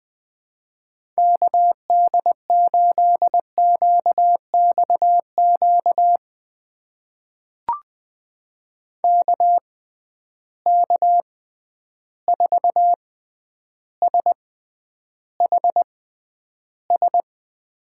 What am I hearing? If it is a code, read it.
Morse code, heard as a keyed tone, about 20 wpm, spaced as for 10 wpm, 700 Hz. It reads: KD8QXQ KK4SHS
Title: USA Call Signs - 6 Characters - 6x Character Spacing - 20wpm